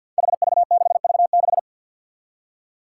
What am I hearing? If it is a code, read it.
Morse code, heard as a keyed tone, about 52 wpm, 700 Hz. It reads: HV6V6
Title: Five Challenging Letter-Number Combinations 50wpm